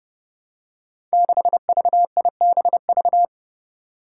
Morse code, heard as a keyed tone, about 29 wpm, 700 Hz. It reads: 6VIBV
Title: Five Challenging Letter-Number Combinations 30wpm